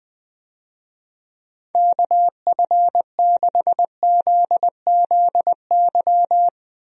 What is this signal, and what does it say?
Morse code, heard as a keyed tone, about 21 wpm, 700 Hz. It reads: KF6ZZY
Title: USA Call Signs - 6 Characters - Repeated 3x - 20wpm